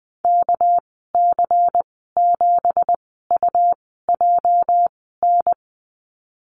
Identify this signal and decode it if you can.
Morse code, heard as a keyed tone, about 20 wpm, spaced as for 14 wpm, 700 Hz. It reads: KC7UJN
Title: 207 Lesson 33g - Letter j - Call Signs 20wpm